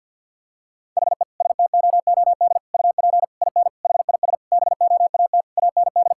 Morse code, heard as a keyed tone, about 50 wpm, 700 Hz. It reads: HE STOOD UP IN HIS BOAT AND
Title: Sentences from Top 1600 Words 50wpm